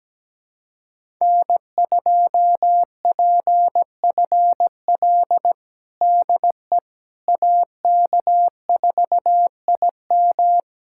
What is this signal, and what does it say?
Morse code, heard as a keyed tone, about 17 wpm, 700 Hz. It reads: N2PFL DE AK4IM
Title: USA Callsign DE Callsign - 5 Characters - 17wpm